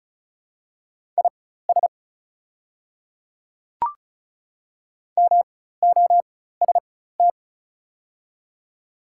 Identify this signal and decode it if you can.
Morse code, heard as a keyed tone, about 34 wpm, spaced as for 16 wpm, 690 Hz. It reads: IS MOST
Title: 4x Character Spacing - Top 100 Words - 35wpm